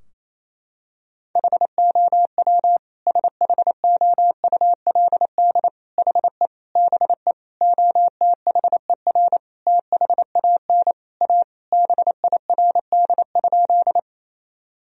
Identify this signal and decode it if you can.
Morse code, heard as a keyed tone, about 27 wpm, 700 Hz. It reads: HOW SHOULD HE BE OTHER THAN A BIRD?
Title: Sentences from Top 1000 Words - Review Part 2 - 28wpm